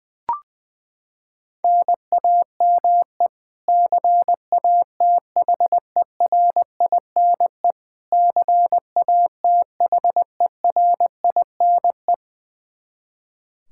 Morse code, heard as a keyed tone, about 20 wpm, 700 Hz. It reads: NAME CATHERINE CATHERINE
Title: Practice QSO - Exchange 1 - Beginning - 20wpm